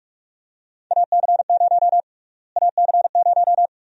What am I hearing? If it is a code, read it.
Morse code, heard as a keyed tone, about 43 wpm, 700 Hz. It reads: AC0 AC0
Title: USA Call Sign Prefixes - Repeated 2x - 45wpm